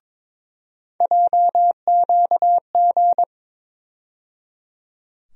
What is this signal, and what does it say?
Morse code, heard as a keyed tone, about 22 wpm, 700 Hz. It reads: JQG
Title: USA Call Sign Postfixes - 3 Letters 22wpm